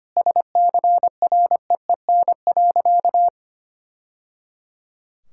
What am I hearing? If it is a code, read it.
Morse code, heard as a keyed tone, about 25 wpm, 700 Hz. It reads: SCREEN.
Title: Sentences from Top 1000 Words - Review Part 6 - 25wpm